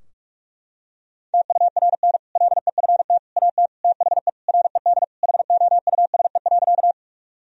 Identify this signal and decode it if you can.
Morse code, heard as a keyed tone, about 46 wpm, 700 Hz. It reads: TURN LEFT AT THE RED HOUSE.
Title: Sentences from Top 1000 Words - Review Part 2 - 45wpm